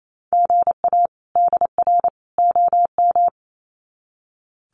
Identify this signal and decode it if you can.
Morse code, heard as a keyed tone, about 28 wpm, 700 Hz. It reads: GA DR OM
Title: Common QSO Phrases 28wpm